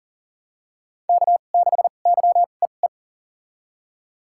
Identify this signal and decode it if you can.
Morse code, heard as a keyed tone, about 40 wpm, spaced as for 29 wpm, 700 Hz. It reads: K6YEE
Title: USA Call Signs - 5 Characters - 2x Character Spacing - 40wpm